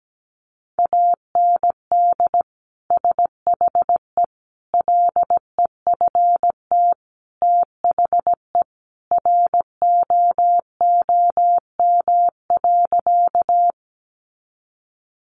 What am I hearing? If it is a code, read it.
Morse code, heard as a keyed tone, about 17 wpm, 700 Hz. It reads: AND SHE LEFT THE ROOM.
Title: Sentences from Top 300 Words 17wpm